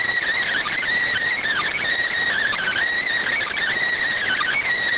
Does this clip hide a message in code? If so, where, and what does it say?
Yes: Emergency Alert System (SAME) tones from 0.4 to 4.6 s, heard as harsh buzzing bursts.